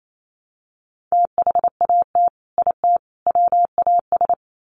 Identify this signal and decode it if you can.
Morse code, heard as a keyed tone, about 28 wpm, 700 Hz. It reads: THAT IT WAS
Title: Sets of 3 Words 28wpm